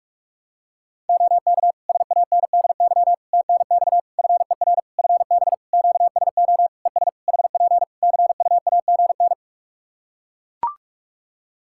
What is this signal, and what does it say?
Morse code, heard as a keyed tone, about 41 wpm, 700 Hz. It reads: OK SANDY TNX FER FB QSO ES HP CUAGN
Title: K1USN SST QSO Practice 45wpm